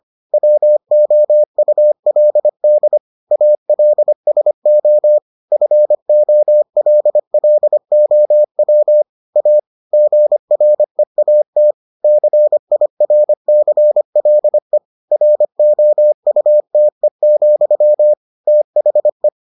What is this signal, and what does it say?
Morse code, heard as a keyed tone, about 25 wpm, 600 Hz. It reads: WOULD ALSO FOLLOW A GREAT CIRCLE ROUTE, THE